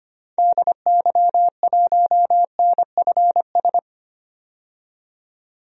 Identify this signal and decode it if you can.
Morse code, heard as a keyed tone, about 25 wpm, 700 Hz. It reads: DY1NFS
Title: DXCC-100 Call Signs - 6 Characters - 25wpm